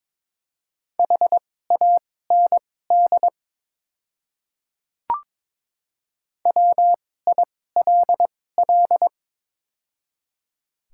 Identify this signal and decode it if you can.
Morse code, heard as a keyed tone, about 22 wpm, spaced as for 16 wpm, 700 Hz. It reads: HAND WILL
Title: 2x Character Spacing - Top 200 Words - 22wpm